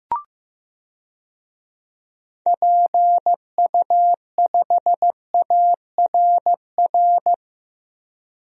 Morse code, heard as a keyed tone, about 15 wpm, 700 Hz. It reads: PU5ARR
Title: DXCC-100 Call Signs - 6 Characters - Repeated 2x - 15wpm